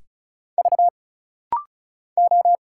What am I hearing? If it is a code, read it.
Morse code, heard as a keyed tone, about 36 wpm, 700 Hz. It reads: V O
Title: Single Letters - Rapid-Fire 35wpm